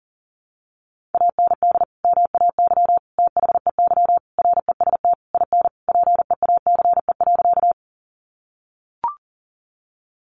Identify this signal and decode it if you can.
Morse code, heard as a keyed tone, about 40 wpm, 700 Hz. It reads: AND MAY THEY REST IN PEACE.
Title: Sentences from Top 1500 Words 40wpm